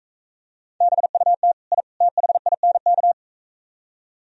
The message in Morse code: BUT I THINK

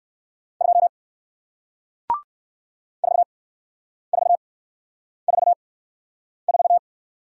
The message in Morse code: 2 4 4 4 4

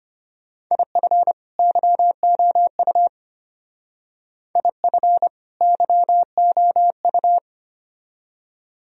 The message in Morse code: IF YOU IF YOU

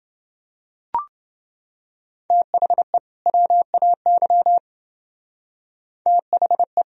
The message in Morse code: THE WAY THE

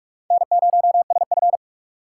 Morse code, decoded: N0SF

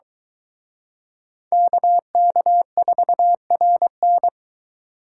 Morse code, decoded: KK4RN

